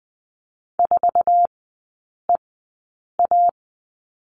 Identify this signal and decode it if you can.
Morse code, heard as a keyed tone, about 20 wpm, spaced as for 8 wpm, 700 Hz. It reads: 4EA
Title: 087 Lesson 15e - Number 5 - Three Characters 20wpm